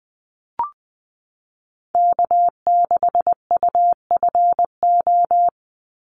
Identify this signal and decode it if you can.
Morse code, heard as a keyed tone, about 19 wpm, 700 Hz. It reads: K6UFO
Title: Call Signs Repeated 3x - 20wpm